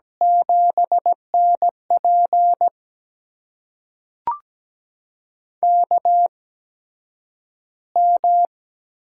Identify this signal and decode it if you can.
Morse code, heard as a keyed tone, about 17 wpm, spaced as for 9 wpm, 700 Hz. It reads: 7NP KM
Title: USA Call Signs - 8x Character Spacing - 17wpm